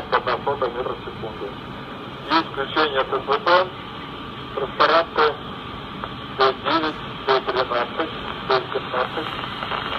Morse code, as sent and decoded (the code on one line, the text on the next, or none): none